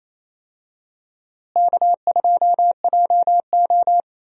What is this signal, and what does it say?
Morse code, heard as a keyed tone, about 28 wpm, 700 Hz. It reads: K2JO
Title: USA Call Signs - 4 Characters - Repeated 2x - 28wpm